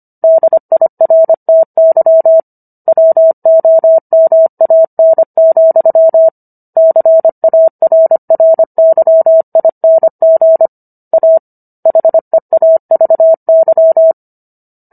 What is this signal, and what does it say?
Morse code, heard as a keyed tone, about 25 wpm, 650 Hz. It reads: DIRTY WOMAN, CARRYING A HEAVY